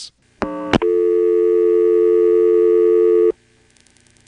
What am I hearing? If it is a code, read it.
Dial tone (North America), 350 + 440 Hz.